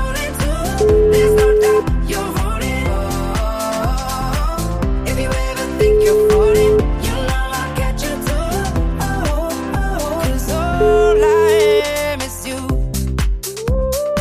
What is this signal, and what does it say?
Ringback tone (Europe, 425 Hz), with noise.